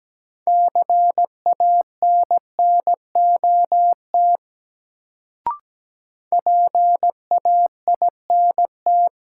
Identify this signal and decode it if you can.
Morse code, heard as a keyed tone, about 17 wpm, 700 Hz. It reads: CANNOT PAINT
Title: Sentences from Top 1000 Words - Review Part 6 - New Word Drill - 17wpm